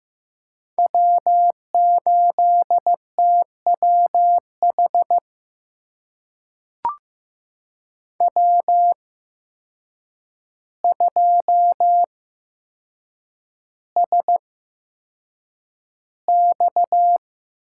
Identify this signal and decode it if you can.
Morse code, heard as a keyed tone, about 15 wpm, spaced as for 7 wpm, 700 Hz. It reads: W8TWH W2SX
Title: USA Call Signs - 8x Character Spacing - 15wpm